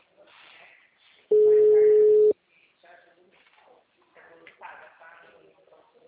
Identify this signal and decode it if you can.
Ringback tone (Europe, 425 Hz), with speech.
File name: out- -2003- - - .6961